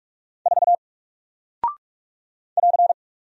Morse code, heard as a keyed tone, about 44 wpm, 700 Hz. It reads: 4 <AR>
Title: Single Character (Letter:Number:Punctuation:Prosign) Alt 45wpm